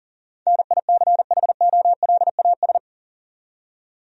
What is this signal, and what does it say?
Morse code, heard as a keyed tone, about 40 wpm, 700 Hz. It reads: NICHOLAS
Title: USA Common Names - 40wpm